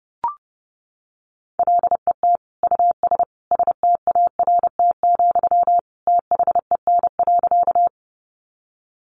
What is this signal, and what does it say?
Morse code, heard as a keyed tone, about 30 wpm, 700 Hz. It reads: LET US START, THEN.
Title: Sentences from Top 300 Words 30wpm